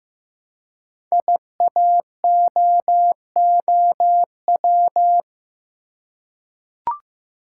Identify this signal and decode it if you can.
Morse code, heard as a keyed tone, about 15 wpm, 700 Hz. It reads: IAOOW
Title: Code Groups - 5 Letters - 15wpm